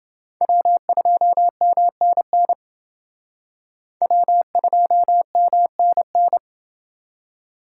Morse code, heard as a keyed tone, about 28 wpm, 700 Hz. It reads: W2MNN W2MNN